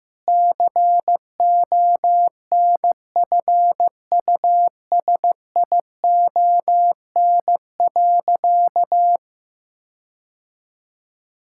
Morse code, heard as a keyed tone, about 15 wpm, 700 Hz. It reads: CONFUSION.
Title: Sentences from Top 1600 Words 15wpm